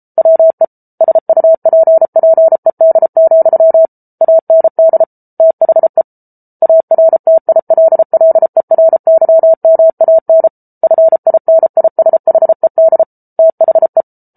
Morse code, heard as a keyed tone, about 34 wpm, 650 Hz. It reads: WE SUPPED, AND THE ARTILLERYMAN FINISHED THE